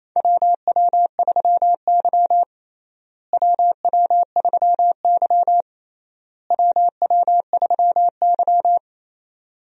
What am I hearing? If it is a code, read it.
Morse code, heard as a keyed tone, about 27 wpm, 700 Hz. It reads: WW3Y WW3Y WW3Y
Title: USA Call Signs - 4 Characters - Repeated 3x - 28wpm